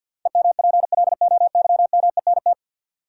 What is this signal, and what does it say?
Morse code, heard as a keyed tone, about 49 wpm, 700 Hz. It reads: EMPLOYMENT
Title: Sentences from Top 1500 Words - New Word Drill Alt - 50wpm